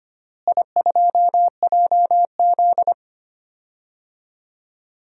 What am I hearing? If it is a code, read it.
Morse code, heard as a keyed tone, about 25 wpm, 700 Hz. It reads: I2JZ
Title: DXCC-100 Call Signs - 4 Characters - Repeated 3x - 25wpm